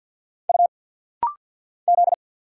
Morse code, heard as a keyed tone, about 50 wpm, 700 Hz. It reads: U Z
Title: Single Letters - Rapid-Fire 50wpm